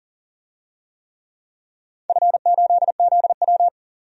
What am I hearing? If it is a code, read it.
Morse code, heard as a keyed tone, about 40 wpm, 700 Hz. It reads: F8ZW
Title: DXCC-100 Call Signs - 4 Characters - 40wpm